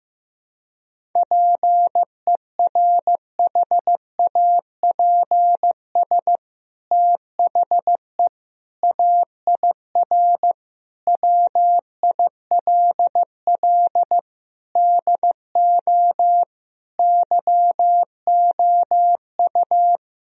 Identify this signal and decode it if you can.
Morse code, heard as a keyed tone, about 15 wpm, 700 Hz. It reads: PERHAPS THE AIR WILL DO YOU
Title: Sentences from Top 1000 Words - Review Part 1 - 15wpm